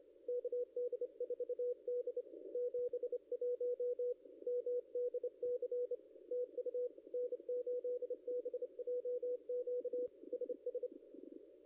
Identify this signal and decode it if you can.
Morse code, heard as a keyed tone, about 28 wpm, 500 Hz. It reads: KD4D 71 MDC TU N8BJQ SS
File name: contests / ss-cw-k5tr-op-n5rz -0600